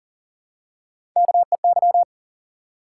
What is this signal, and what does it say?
Morse code, heard as a keyed tone, about 40 wpm, 700 Hz. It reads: KEY
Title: Three Letter Words - 40wpm